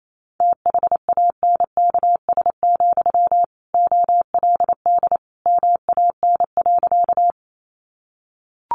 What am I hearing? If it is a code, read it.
Morse code, heard as a keyed tone, about 28 wpm, 700 Hz. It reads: THANKS, OLD MAN.
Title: Sentences from Top 600 Words 28wpm